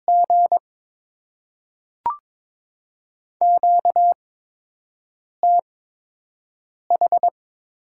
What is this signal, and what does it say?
Morse code, heard as a keyed tone, about 22 wpm, 700 Hz. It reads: G Q T H
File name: Top 20 QSO Elements - 8x Character Spacing - 22wpm